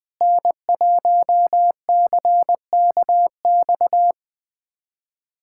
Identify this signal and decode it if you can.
Morse code, heard as a keyed tone, about 20 wpm, 700 Hz. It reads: N1CKX
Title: USA Callsign DE Callsign - 5 Characters - 20wpm